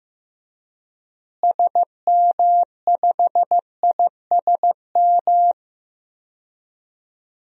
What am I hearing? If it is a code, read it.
Morse code, heard as a keyed tone, about 15 wpm, 700 Hz. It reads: SM5ISM
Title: DXCC-100 Call Signs - 6 Characters - Repeated 3x - 15wpm